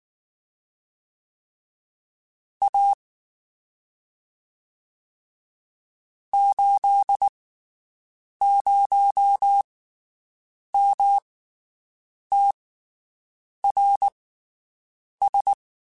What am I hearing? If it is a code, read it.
Morse code, heard as a keyed tone, about 19 wpm, spaced as for 7 wpm, 800 Hz. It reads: A 80MTRS
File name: QSOs Spaced